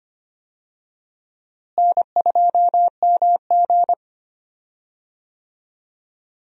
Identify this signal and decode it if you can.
Morse code, heard as a keyed tone, about 25 wpm, 700 Hz. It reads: N2MG
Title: USA Call Signs - 4 Characters - Repeated 3x - 25wpm